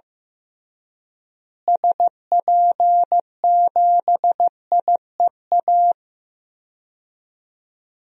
Morse code, heard as a keyed tone, about 15 wpm, 700 Hz. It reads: SP7IEA
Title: DXCC-100 Call Signs - 6 Characters - Repeated 3x - 15wpm